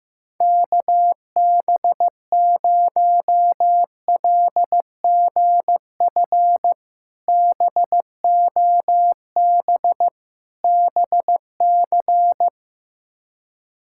Morse code, heard as a keyed tone, about 15 wpm, 700 Hz. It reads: KB0LGF BOB BC
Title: K1USN SST QSO Practice - 15wpm